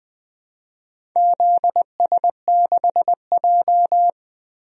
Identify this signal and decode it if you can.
Morse code, heard as a keyed tone, about 20 wpm, 700 Hz. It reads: ZS6J
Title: DXCC-100 Call Signs - 4 Characters - 20wpm